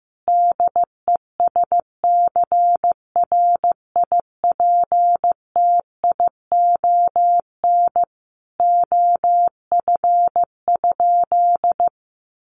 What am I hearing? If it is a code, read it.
Morse code, heard as a keyed tone, about 15 wpm, 700 Hz. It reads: DESCRIPTION OF?
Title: Sentences from Top 1300 Words 15wpm